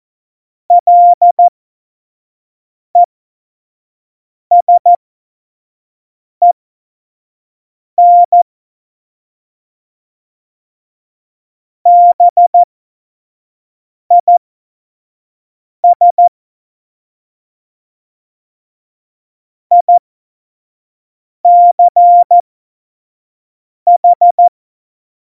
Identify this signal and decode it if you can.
Morse code, heard as a keyed tone, about 14 wpm, spaced as for 5 wpm, 700 Hz. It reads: LESEN BIS ICH
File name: Morsekurs / Witze / Witz 28. November 2017, 25 F70